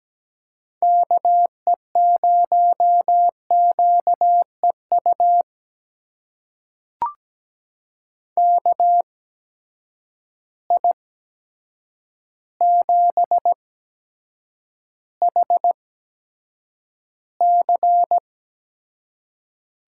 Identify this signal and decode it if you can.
Morse code, heard as a keyed tone, about 17 wpm, spaced as for 8 wpm, 700 Hz. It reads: KE0QEU KI7HC
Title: USA Call Signs - 6 Characters - 8x Character Spacing - 17wpm